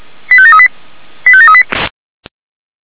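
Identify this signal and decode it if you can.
Five-tone selective call (ZVEI), heard as quick runs of five tones, tones of about 70 ms. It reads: 85738 85738
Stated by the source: Received on an SDR; ZVEI five-tone selective call.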